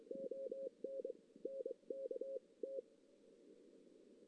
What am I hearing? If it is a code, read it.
Morse code, heard as a keyed tone, about 23 wpm, 540 Hz. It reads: ON N K T
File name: 2016 JAN CW NAQP / 2016 Jan naqp k5ot - 0-3